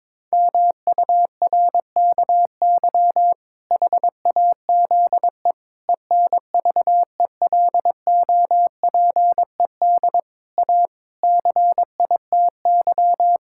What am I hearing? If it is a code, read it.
Morse code, heard as a keyed tone, about 22 wpm, 700 Hz. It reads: MURKY HAZE ENVELOPED A CITY
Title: Pangram Challenge v2 - 22wpm